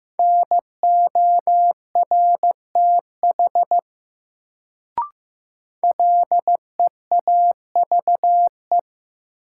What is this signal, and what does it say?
Morse code, heard as a keyed tone, about 15 wpm, 700 Hz. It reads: NORTH LEAVE
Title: Sentences from Top 400 Words - New Word Drill - 15wpm